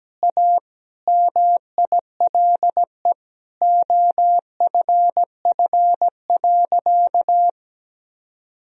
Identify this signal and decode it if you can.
Morse code, heard as a keyed tone, about 17 wpm, 700 Hz. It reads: A MILE OFF.
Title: Sentences from Top 1000 Words - Review Part 1 - 17wpm